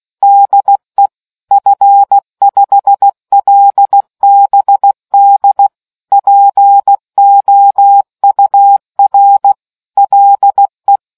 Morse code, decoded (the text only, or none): DE F5LBD POUR LE